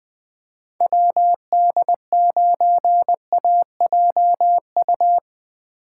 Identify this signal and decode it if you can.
Morse code, heard as a keyed tone, about 20 wpm, 700 Hz. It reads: WD9AJU